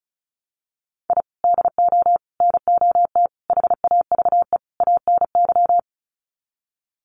Morse code, heard as a keyed tone, about 34 wpm, 700 Hz. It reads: I DO NOT HAVE ANY
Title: Sets of 5 Words 35wpm